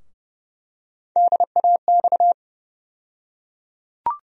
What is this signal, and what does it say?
Morse code, heard as a keyed tone, about 30 wpm, 700 Hz. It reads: DAX
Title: USA Call Sign Postfixes - 3 Letters 30wpm